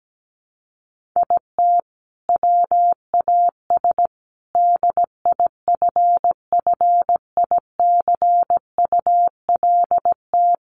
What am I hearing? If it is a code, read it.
Morse code, heard as a keyed tone, about 17 wpm, 700 Hz. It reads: IT WAS DIFFICULT
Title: Sentences from Top 1400 Words 17wpm